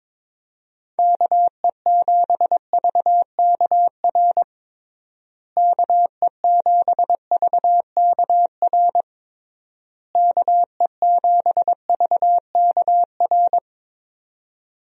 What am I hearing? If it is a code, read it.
Morse code, heard as a keyed tone, about 22 wpm, 700 Hz. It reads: KE7VKR KE7VKR KE7VKR
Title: USA Call Signs - 6 Characters - Repeated 3x - 22wpm